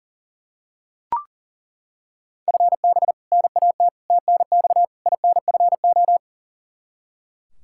Morse code, heard as a keyed tone, about 40 wpm, 700 Hz. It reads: FB NAT TNX INFO
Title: Practice QSO - Exchange 3 - More Info - 40wpm